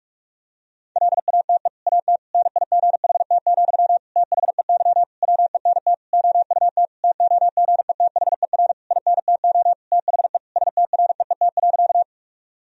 Morse code, heard as a keyed tone, about 41 wpm, 700 Hz. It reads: LATE AT NIGHT, THEY WENT OUT TOGETHER INTO THE STREET.
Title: Sentences from Top 1000 Words - Review Part 3 - 45wpm